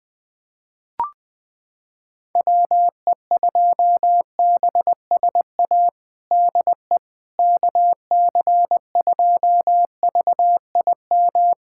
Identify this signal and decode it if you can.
Morse code, heard as a keyed tone, about 20 wpm, 700 Hz. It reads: WE2BSA DE KC2VIM
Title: USA Callsign DE Callsign - 6 Characters - 20wpm